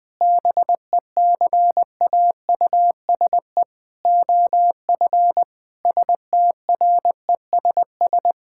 Morse code, heard as a keyed tone, about 21 wpm, 700 Hz. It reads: BECAUSE OF STRESS
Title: Sentences from Top 1000 Words - Review Part 6 - 20wpm